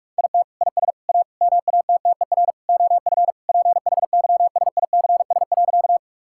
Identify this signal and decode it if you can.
Morse code, heard as a keyed tone, about 44 wpm, 700 Hz. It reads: IT IS A MATTER OF PHYSICS.
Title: Sentences from Top 1000 Words - Review Part 5 - 45wpm